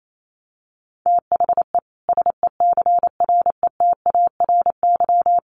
Morse code, heard as a keyed tone, about 28 wpm, 700 Hz. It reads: THE SECRETARY
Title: Sentences from Top 1400 Words 28wpm